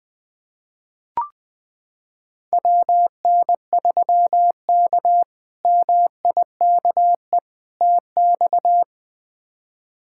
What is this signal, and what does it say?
Morse code, heard as a keyed tone, about 20 wpm, 700 Hz. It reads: WN3K MIKE TX
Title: K1USN SST QSO Practice - 20wpm